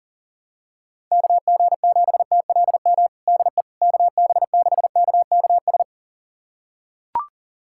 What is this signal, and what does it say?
Morse code, heard as a keyed tone, about 40 wpm, 700 Hz. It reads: KG7TLM DE KB6KKS
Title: USA Callsign DE Callsign - 6 Characters - 40wpm